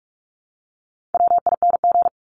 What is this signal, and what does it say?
Morse code, heard as a keyed tone, about 44 wpm, 700 Hz. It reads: WING
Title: Top 1001-1250 Words 45wpm